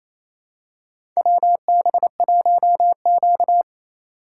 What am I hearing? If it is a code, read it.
Morse code, heard as a keyed tone, about 28 wpm, 700 Hz. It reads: WB1Q
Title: USA Call Signs - 4 Characters - 6x Character Spacing - 28wpm